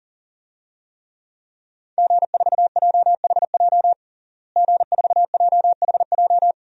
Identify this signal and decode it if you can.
Morse code, heard as a keyed tone, about 40 wpm, 700 Hz. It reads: G4JHJ G4JHJ
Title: DXCC-100 Call Signs - 5 Characters - Repeated 3x - 40wpm